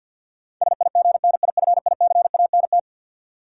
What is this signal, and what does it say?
Morse code, heard as a keyed tone, about 52 wpm, 700 Hz. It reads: SIGNIFICANT
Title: Sentences from Top 1000 Words - Review Part 6 - New Word Drill - 50wpm